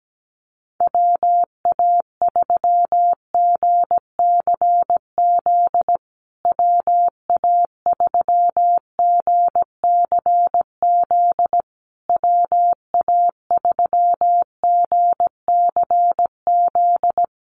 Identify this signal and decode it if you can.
Morse code, heard as a keyed tone, about 17 wpm, 700 Hz. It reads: WA3GCZ WA3GCZ WA3GCZ
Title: USA Call Signs Repeated 3x 17wpm